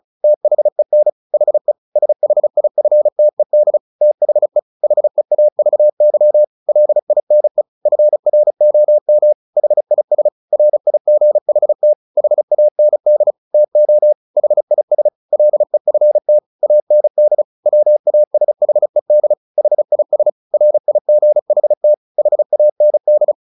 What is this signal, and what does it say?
Morse code, heard as a keyed tone, about 34 wpm, 600 Hz. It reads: THEN HE SHIFTED THE HEAVY LINE FROM HIS RIGHT HAND TO HIS LEFT AND WASHED HIS RIGHT HAND
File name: the old man and the sea 2 35wpm